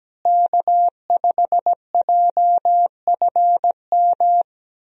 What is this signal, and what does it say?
Morse code, heard as a keyed tone, about 17 wpm, 700 Hz. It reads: K5JFM
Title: USA Call Signs - 5 Characters - Repeated 3x - 17wpm